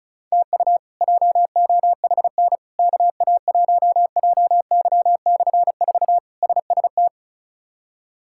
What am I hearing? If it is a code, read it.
Morse code, heard as a keyed tone, about 34 wpm, 700 Hz. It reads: TU JOHN KA1JY/4 SST
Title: K1USN SST QSO Practice - 35wpm